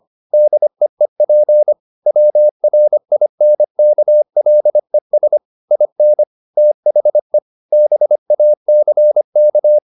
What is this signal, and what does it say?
Morse code, heard as a keyed tone, about 25 wpm, 600 Hz. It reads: DEEP WRINKLES IN THE BACK
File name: the old man and the sea part 1 25wpm